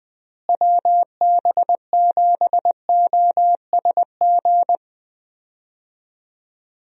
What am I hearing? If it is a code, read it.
Morse code, heard as a keyed tone, about 20 wpm, 700 Hz. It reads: WB7OSG
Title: USA Call Signs - 6 Characters - Repeated 3x - 20wpm